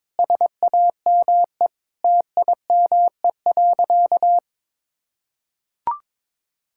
Morse code, heard as a keyed tone, about 22 wpm, 700 Hz. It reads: SAME TIME.
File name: practice / 22wpm / Sentences with 8-Word Phrases - 22wpm